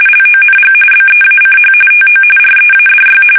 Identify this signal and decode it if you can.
RTTY, about 45.45 baud. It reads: LONG BEACH TO READY
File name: fsk850 / battleshipnj